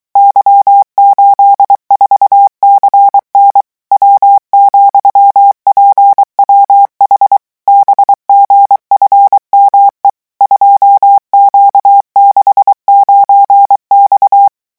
Morse code, heard as a keyed tone, about 23 wpm, 800 Hz. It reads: Y84CN W,PWH BGFME 2Q69X
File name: morse mp3 / revision13-vitesse 22 mots